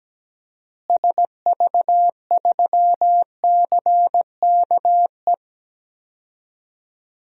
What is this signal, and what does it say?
Morse code, heard as a keyed tone, about 17 wpm, 700 Hz. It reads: SV3CKE
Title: DXCC-100 Call Signs - 6 Characters - Repeated 3x - 17wpm